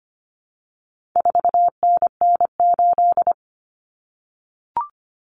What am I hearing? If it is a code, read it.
Morse code, heard as a keyed tone, about 25 wpm, 700 Hz. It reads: 4NN8
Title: Common Cut Numbers - 4-Digit 25wpm